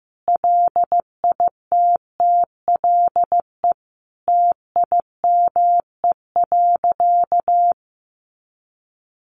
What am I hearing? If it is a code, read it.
Morse code, heard as a keyed tone, about 15 wpm, 700 Hz. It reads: LITTLE TIME.
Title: Sentences from Top 200 Words 15wpm